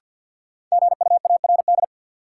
Morse code, heard as a keyed tone, about 47 wpm, 700 Hz. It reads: GUARD
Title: Sentences from Top 700 Words - New Word Drill - 50wpm